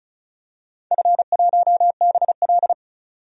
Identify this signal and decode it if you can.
Morse code, heard as a keyed tone, about 36 wpm, 700 Hz. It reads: F1BL
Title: DXCC-100 Call Signs - 4 Characters - 35wpm